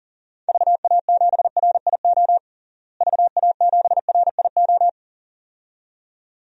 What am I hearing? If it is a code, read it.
Morse code, heard as a keyed tone, about 37 wpm, 700 Hz. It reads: VA7RIO VA7RIO
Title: DXCC-100 Call Signs - 6 Characters - Repeated 2x - 40wpm